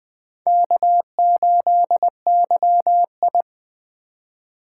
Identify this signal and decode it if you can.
Morse code, heard as a keyed tone, about 20 wpm, 700 Hz. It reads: K8YI
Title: USA Call Signs - 4x Character Spacing - 20wpm